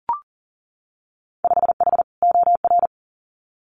Morse code, heard as a keyed tone, about 39 wpm, 700 Hz. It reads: 5H OR
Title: Winter Field Day Exchanges Encore 40wpm